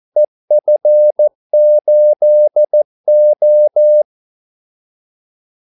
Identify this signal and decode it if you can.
Morse code, heard as a keyed tone, about 14 wpm, 600 Hz. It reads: EF8O
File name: Callsigns-14wpm